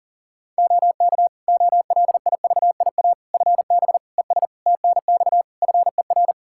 Morse code, heard as a keyed tone, about 40 wpm, 700 Hz. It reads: OK OLIVIA FB ES TNX FER
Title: Practice QSO - Exchange 2 - Info - 40wpm